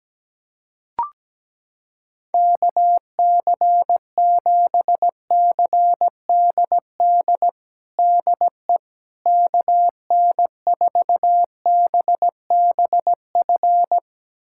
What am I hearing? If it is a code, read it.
Morse code, heard as a keyed tone, about 17 wpm, 700 Hz. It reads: KC7CDD DE KN4BBF
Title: USA Callsign DE Callsign - 6 Characters - 17wpm